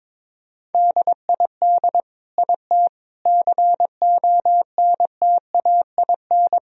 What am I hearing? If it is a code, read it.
Morse code, heard as a keyed tone, about 21 wpm, 700 Hz. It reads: DID IT CONTAIN
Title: Sentences from Top 1000 Words - Review Part 2 - 22wpm